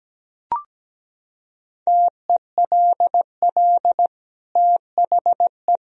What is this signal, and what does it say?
Morse code, heard as a keyed tone, about 17 wpm, 700 Hz. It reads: TELL THE
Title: Sentences from Top 1000 Words - Review Part 2 - 17wpm